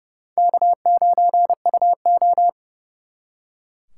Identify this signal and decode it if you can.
Morse code, heard as a keyed tone, about 37 wpm, 700 Hz. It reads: K9UO
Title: Speed Racing - USA 1x2 Call Signs 30wpm